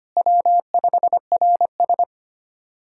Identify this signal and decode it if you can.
Morse code, heard as a keyed tone, about 25 wpm, 700 Hz. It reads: W5RS
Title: USA Callsign DE Callsign 25wpm